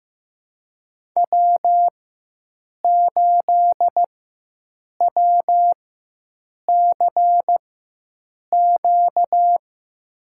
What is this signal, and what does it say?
Morse code, heard as a keyed tone, about 15 wpm, spaced as for 7 wpm, 700 Hz. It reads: W8WCQ
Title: USA Call Signs - 5 Characters - 4x Character Spacing - 15wpm